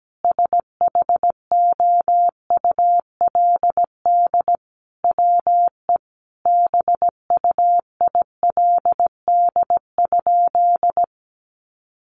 Morse code, heard as a keyed tone, about 17 wpm, 700 Hz. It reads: SHOULD WE BUILD?
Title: Sentences from Top 1300 Words 17wpm